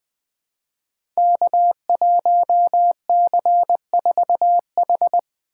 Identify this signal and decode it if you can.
Morse code, heard as a keyed tone, about 20 wpm, 700 Hz. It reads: K1C4H
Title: Code Groups - 5 Letter:Numbers - 20wpm